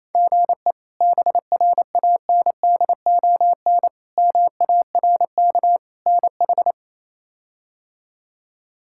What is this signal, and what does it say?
Morse code, heard as a keyed tone, about 27 wpm, 700 Hz. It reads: GE BRANDON MARK NH
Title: K1USN SST QSO Practice - 28wpm